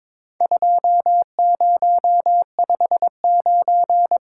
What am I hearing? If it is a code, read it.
Morse code, heard as a keyed tone, about 22 wpm, 700 Hz. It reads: 2059
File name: practice / 22wpm / Four Letter-Number Combinations - 22wpm